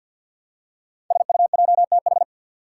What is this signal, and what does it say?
Morse code, heard as a keyed tone, about 53 wpm, 700 Hz. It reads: SUJTH
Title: Code Groups - 5 Letter:Numbers - 50wpm